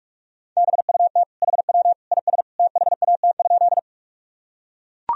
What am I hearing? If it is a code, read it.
Morse code, heard as a keyed tone, about 42 wpm, 700 Hz. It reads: BUT HW IS THAT?
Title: Sentences from Top 100 Words and Abbreviations 45wpm